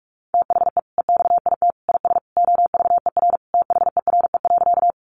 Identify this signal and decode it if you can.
Morse code, heard as a keyed tone, about 46 wpm, 700 Hz. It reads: THE EXIT IS OVER THERE.
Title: Sentences from Top 1100 Words 45wpm